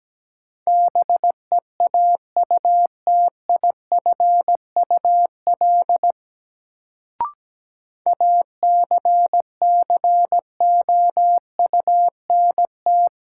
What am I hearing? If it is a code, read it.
Morse code, heard as a keyed tone, about 17 wpm, 700 Hz. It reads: BEAUTIFUL ACCOUNT
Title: Sentences from Top 900 Words - New Word Drill - 17wpm